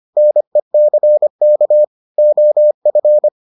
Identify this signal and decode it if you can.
Morse code, heard as a keyed tone, about 25 wpm, 600 Hz. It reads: NECK OF